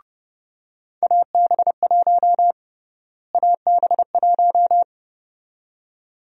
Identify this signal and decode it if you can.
Morse code, heard as a keyed tone, about 30 wpm, 700 Hz. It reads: AB1 AB1
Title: USA Call Sign Prefixes - Repeated 2x - 30wpm